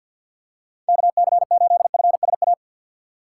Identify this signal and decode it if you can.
Morse code, heard as a keyed tone, about 50 wpm, 700 Hz. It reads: KC8FSA
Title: USA Call Signs - 2x Character Spacing - 50wpm